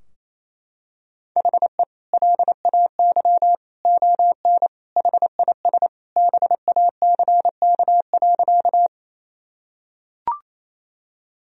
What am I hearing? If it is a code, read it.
Morse code, heard as a keyed tone, about 28 wpm, 700 Hz. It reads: HE LAY ON HIS BACK.
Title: Sentences from Top 1000 Words - Review Part 2 - 28wpm